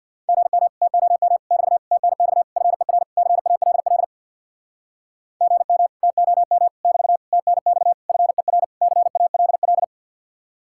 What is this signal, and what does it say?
Morse code, heard as a keyed tone, about 64 wpm, 700 Hz. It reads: GM TOM <BT> TNX FER CALL GM TOM <BT> TNX FER CALL